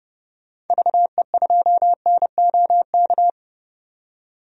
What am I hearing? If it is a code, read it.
Morse code, heard as a keyed tone, about 30 wpm, 700 Hz. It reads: VE2NOK
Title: DXCC-100 Call Signs - 6 Characters - 30wpm